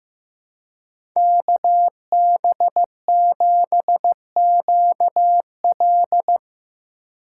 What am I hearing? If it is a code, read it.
Morse code, heard as a keyed tone, about 17 wpm, 700 Hz. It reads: KB7QL
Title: Speed Racing - USA 2x2 Call Signs 15wpm